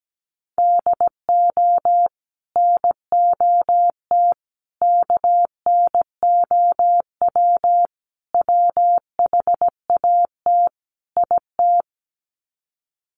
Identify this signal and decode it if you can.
Morse code, heard as a keyed tone, about 17 wpm, 700 Hz. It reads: DO NOT KNOW WHAT IT
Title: Sets of 5 Words 17wpm